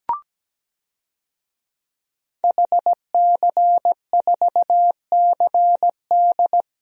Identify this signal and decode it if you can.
Morse code, heard as a keyed tone, about 17 wpm, 700 Hz. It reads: HC4CD
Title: DXCC-100 Call Signs - 5 Characters - 17wpm